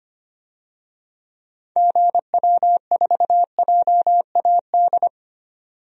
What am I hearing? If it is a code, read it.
Morse code, heard as a keyed tone, about 25 wpm, 700 Hz. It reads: GW4JAD
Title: DXCC-100 Call Signs - 6 Characters - 25wpm